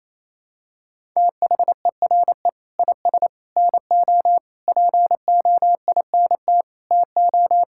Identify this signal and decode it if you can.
Morse code, heard as a keyed tone, about 28 wpm, 700 Hz. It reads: THERE IS NO POINT TO
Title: Sentences from Top 1000 Words - Review Part 5 - 28wpm